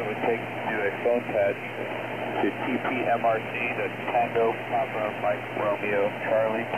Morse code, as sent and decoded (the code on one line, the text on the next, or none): none